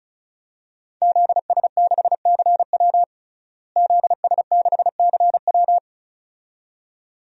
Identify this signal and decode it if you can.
Morse code, heard as a keyed tone, about 36 wpm, 700 Hz. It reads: ZS6CW ZS6CW
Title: DXCC-100 Call Signs - 5 Characters - Repeated 2x - 35wpm